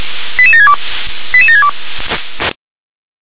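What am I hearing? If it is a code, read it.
Five-tone selective call (ZVEI), heard as quick runs of five tones, tones of about 70 ms. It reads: 88762 88762
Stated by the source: ZVEI five-tone selective call received by SDR.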